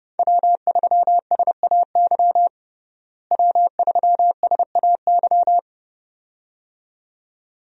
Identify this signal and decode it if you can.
Morse code, heard as a keyed tone, about 30 wpm, 700 Hz. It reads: W3SAY W3SAY
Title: USA Call Signs - 5 Characters - Repeated 2x - 30wpm